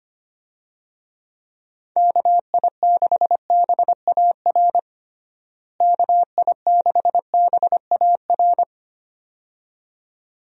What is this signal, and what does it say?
Morse code, heard as a keyed tone, about 25 wpm, 700 Hz. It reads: KI6BAR KI6BAR
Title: USA Call Signs - 6 Characters - Repeated 2x - 25wpm